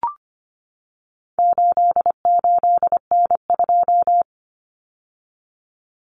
Morse code, heard as a keyed tone, about 25 wpm, 700 Hz. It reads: 88N2
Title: Common Cut Numbers - 4-Digit 25wpm